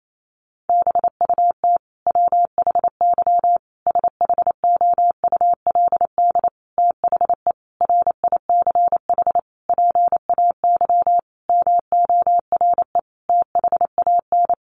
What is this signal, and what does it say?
Morse code, heard as a keyed tone, about 28 wpm, 700 Hz. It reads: BUT WHY SHOULD THE RICH PAY MORE THAN
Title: Sentences from Top 1100 Words 28wpm